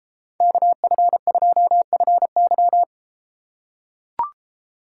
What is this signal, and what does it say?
Morse code, heard as a keyed tone, about 32 wpm, 700 Hz. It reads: KF2FY